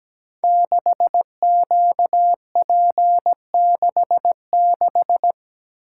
Morse code, heard as a keyed tone, about 17 wpm, 700 Hz. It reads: 6QP66
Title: Code Groups - 5 Letter:Numbers - 17wpm